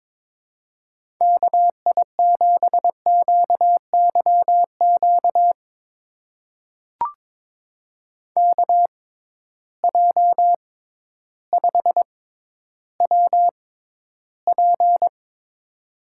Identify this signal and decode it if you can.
Morse code, heard as a keyed tone, about 22 wpm, spaced as for 12 wpm, 700 Hz. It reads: KI7QYQ KJ5WP
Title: USA Call Signs - 6x Character Spacing - 22wpm